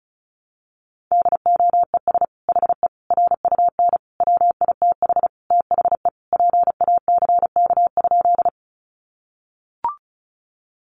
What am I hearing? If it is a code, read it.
Morse code, heard as a keyed tone, about 34 wpm, 700 Hz. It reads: DOES HE RUN WITH THE PACK?
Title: Sentences from Top 1300 Words 35wpm